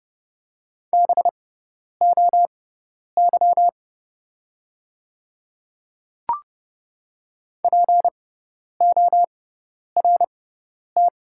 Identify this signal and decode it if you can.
Morse code, heard as a keyed tone, about 32 wpm, spaced as for 10 wpm, 700 Hz. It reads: BOY PORT
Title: 6x Character Spacing - Top 300 Words - 30wpm